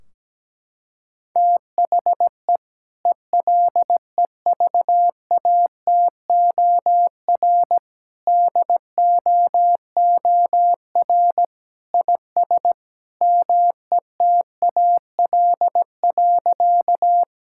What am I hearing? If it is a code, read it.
Morse code, heard as a keyed tone, about 17 wpm, 700 Hz. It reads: THE ELEVATOR DOOR IS METAL.
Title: Sentences from Top 1600 Words 17wpm